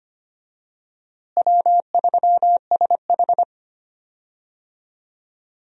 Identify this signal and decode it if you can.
Morse code, heard as a keyed tone, about 25 wpm, 700 Hz. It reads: W3SH
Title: USA Call Signs - 4 Characters - Repeated 3x - 25wpm